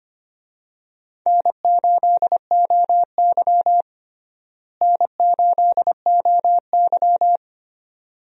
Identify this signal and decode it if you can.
Morse code, heard as a keyed tone, about 25 wpm, 700 Hz. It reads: N8OY N8OY
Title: USA Call Signs - 4 Characters - Repeated 2x - 25wpm